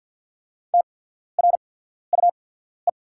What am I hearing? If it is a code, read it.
Morse code, heard as a keyed tone, about 48 wpm, spaced as for 14 wpm, 690 Hz. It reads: TRUE
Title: 8x Character Spacing - Top 500 Words - 50wpm